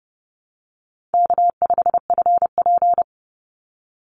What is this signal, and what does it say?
Morse code, heard as a keyed tone, about 30 wpm, 700 Hz. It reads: K5FP
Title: Call Signs - 30wpm